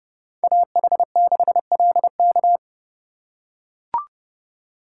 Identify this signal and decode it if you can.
Morse code, heard as a keyed tone, about 29 wpm, 700 Hz. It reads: AH6LK